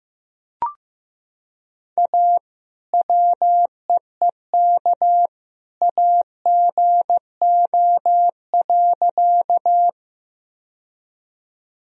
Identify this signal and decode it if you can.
Morse code, heard as a keyed tone, about 15 wpm, 700 Hz. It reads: A WEEK AGO.
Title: Sentences from Top 1000 Words - Review Part 3 - 15wpm